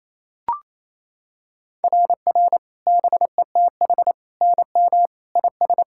Morse code, heard as a keyed tone, about 27 wpm, 690 Hz. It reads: RR BETH NM IS
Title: SOTA QSO Practice 28wpm